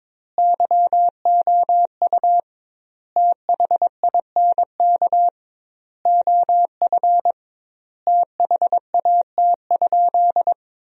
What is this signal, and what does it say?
Morse code, heard as a keyed tone, about 22 wpm, 700 Hz. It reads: YOU THINK OF THAT?
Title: 2x Word-Spacing - Sentences from Top 200 Words - 22wpm